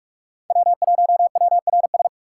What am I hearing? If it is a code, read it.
Morse code, heard as a keyed tone, about 44 wpm, 700 Hz. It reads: W1WRS